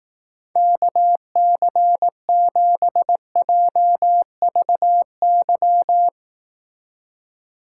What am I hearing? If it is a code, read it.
Morse code, heard as a keyed tone, about 18 wpm, 700 Hz. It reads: KC7JVY
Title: Speed Racing - USA 2x3 Call Signs 15wpm